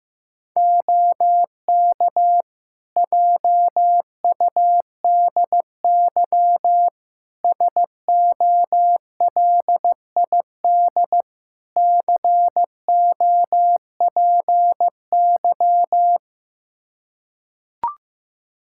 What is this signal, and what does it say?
Morse code, heard as a keyed tone, about 15 wpm, 700 Hz. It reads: OK JUDY SOLID COPY
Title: Practice QSO - Exchange 3 - More Info - 15wpm